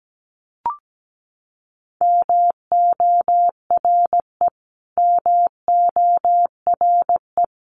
Morse code, heard as a keyed tone, about 17 wpm, 700 Hz. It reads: MORE MORE
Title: Top 500 Words Repeated 17wpm